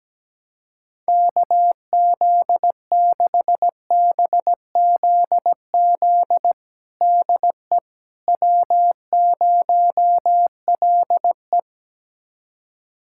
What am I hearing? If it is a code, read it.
Morse code, heard as a keyed tone, about 17 wpm, 700 Hz. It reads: KZ6BZZ DE W0LE
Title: USA Callsign DE Callsign 17wpm